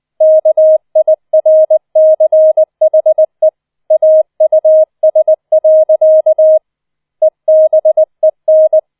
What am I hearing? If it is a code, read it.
Morse code, heard as a keyed tone, about 20 wpm, 630 Hz. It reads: KIRCHE AUS. EBEN